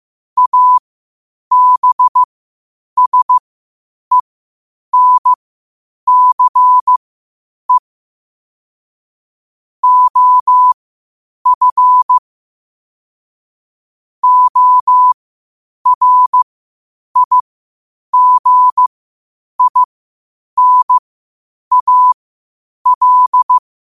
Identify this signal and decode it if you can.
Morse code, heard as a keyed tone, about 15 wpm, spaced as for 8 wpm, 1000 Hz. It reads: ABSENCE OF ORIGINAL
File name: qotdfiles / 8wpm